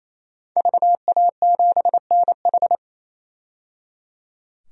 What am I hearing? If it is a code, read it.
Morse code, heard as a keyed tone, about 28 wpm, 700 Hz. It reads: VA7NH
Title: DXCC-100 Call Signs - 5 Characters - 28wpm